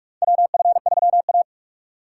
Morse code, heard as a keyed tone, about 46 wpm, 700 Hz. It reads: WF3A
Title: USA Call Signs - 4 Characters - 4x Character Spacing - 45wpm